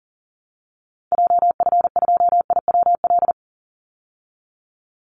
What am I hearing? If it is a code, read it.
Morse code, heard as a keyed tone, about 40 wpm, 700 Hz. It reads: JF2IWL
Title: Call Signs Repeated 2x - 40wpm